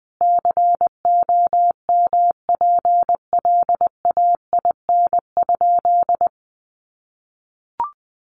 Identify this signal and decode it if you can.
Morse code, heard as a keyed tone, about 20 wpm, 700 Hz. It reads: COMPLAIN?
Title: Sentences from Top 1300 Words 20wpm